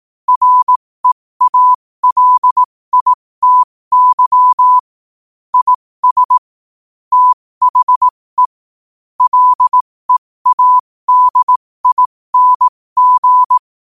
Morse code, heard as a keyed tone, about 18 wpm, spaced as for 15 wpm, 1000 Hz. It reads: REALITY IS THE LEADING
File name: qotdfiles / 15wpm